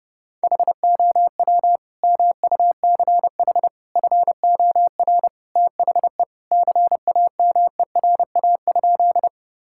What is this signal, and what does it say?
Morse code, heard as a keyed tone, about 30 wpm, 700 Hz. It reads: HOW MUCH FOR THE CAMERA?
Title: Sentences from Top 1000 Words - Review Part 5 - 30wpm